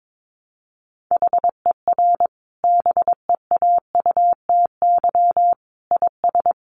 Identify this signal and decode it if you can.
Morse code, heard as a keyed tone, about 22 wpm, 700 Hz. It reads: HER BEAUTY IS
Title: Sentences from Top 1100 Words 22wpm